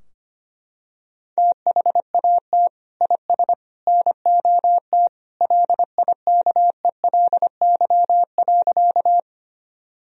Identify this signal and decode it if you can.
Morse code, heard as a keyed tone, about 25 wpm, 700 Hz. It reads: THAT IS NOT LIKELY.
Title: Sentences from Top 1000 Words - Review Part 4 - 25wpm